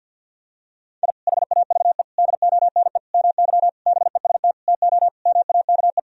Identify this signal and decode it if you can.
Morse code, heard as a keyed tone, about 50 wpm, 700 Hz. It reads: I HAVE DONE MY BEST TO MAKE
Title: Sentences from Top 1000 Words - Review Part 4 - 50wpm